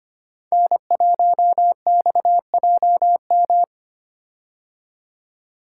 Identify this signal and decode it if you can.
Morse code, heard as a keyed tone, about 25 wpm, 700 Hz. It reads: N1XJM
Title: K1USN SST QSO Practice - 25wpm